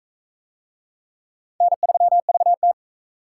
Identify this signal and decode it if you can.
Morse code, heard as a keyed tone, about 38 wpm, 700 Hz. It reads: N3VT